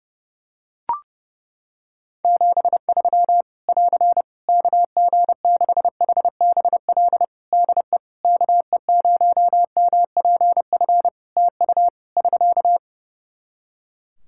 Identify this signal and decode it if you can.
Morse code, heard as a keyed tone, about 30 wpm, 700 Hz. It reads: 73 <AR> KG6HBL DE KE0MPF TU <SK>
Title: Practice QSO - Exchange 4 - Ending - 30wpm